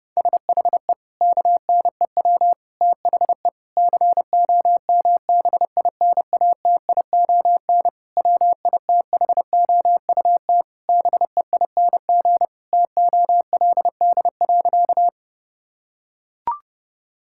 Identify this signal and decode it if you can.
Morse code, heard as a keyed tone, about 32 wpm, 700 Hz. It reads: SHE KNEW THE COMBINATION WITHOUT BEING TOLD.
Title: Sentences from Top 1000 Words - Review Part 6 - 30wpm